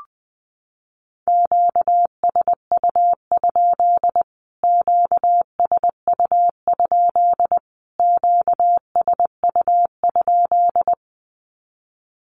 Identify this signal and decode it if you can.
Morse code, heard as a keyed tone, about 20 wpm, 700 Hz. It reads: QSU? QSU? QSU?
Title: Q-Codes Repeated 3x 20wpm